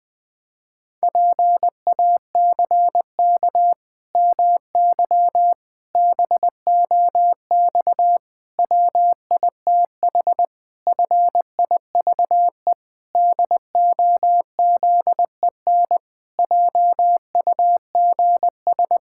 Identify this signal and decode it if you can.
Morse code, heard as a keyed tone, about 19 wpm, 700 Hz. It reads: PACK MY BOX WITH FIVE DOZEN JUGS
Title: Pangram Challenge v2 - 20wpm